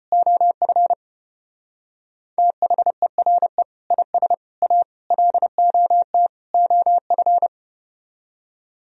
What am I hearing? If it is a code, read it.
Morse code, heard as a keyed tone, about 33 wpm, 700 Hz. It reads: OF THERE IS A LOT OF